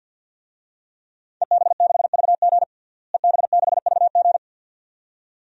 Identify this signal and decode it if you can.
Morse code, heard as a keyed tone, about 52 wpm, 700 Hz. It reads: EB6VG EB6VG
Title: DXCC-100 Call Signs - 5 Characters - Repeated 2x - 50wpm